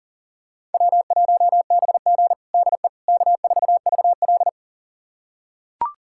Morse code, heard as a keyed tone, about 37 wpm, 700 Hz. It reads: W1BG DE K4VL
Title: USA Callsign DE Callsign - 4 Characters - 40wpm